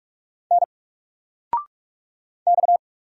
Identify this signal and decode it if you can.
Morse code, heard as a keyed tone, about 46 wpm, 700 Hz. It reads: N X
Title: Single Letter-Number 45wpm